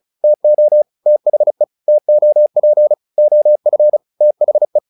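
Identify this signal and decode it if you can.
Morse code, heard as a keyed tone, about 34 wpm, 600 Hz. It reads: TO THE TOP OF THE
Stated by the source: Morse practice text "Tower Trouble" sent at 35 WPM, from the SKCC - Straight Key Century Club learning center audio files.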